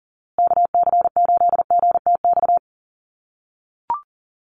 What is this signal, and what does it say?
Morse code, heard as a keyed tone, about 40 wpm, 700 Hz. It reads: KC8GTX
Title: USA Call Signs Repeated 2x 40wpm